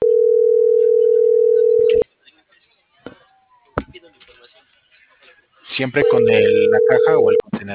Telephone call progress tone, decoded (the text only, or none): ringback tone